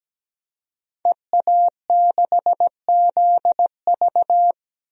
Morse code, heard as a keyed tone, about 17 wpm, 700 Hz. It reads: EA6ZV
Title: DXCC-100 Call Signs - 5 Characters - 17wpm